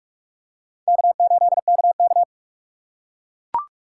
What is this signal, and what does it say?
Morse code, heard as a keyed tone, about 38 wpm, 700 Hz. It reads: K8KK